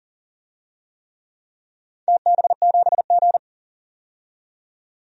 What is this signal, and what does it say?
Morse code, heard as a keyed tone, about 40 wpm, 690 Hz. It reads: TB7G
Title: DXCC-100 Call Signs - 4 Characters - 40wpm